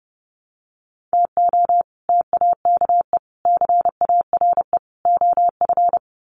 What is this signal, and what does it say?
Morse code, heard as a keyed tone, about 30 wpm, 700 Hz. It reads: TO TAKE CARE OF
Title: Sets of 4 Words 30wpm